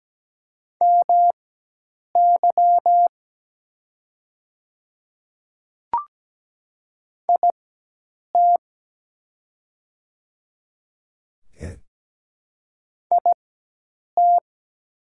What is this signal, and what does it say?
Morse code, heard as a keyed tone, about 17 wpm, spaced as for 8 wpm, 700 Hz. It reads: MY IT IT
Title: 4x Character Spacing - Top 100 Words - 17wpm